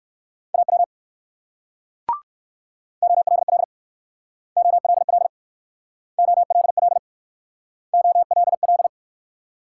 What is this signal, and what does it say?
Morse code, heard as a keyed tone, about 59 wpm, 690 Hz. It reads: AW OLL OLL OLL OLL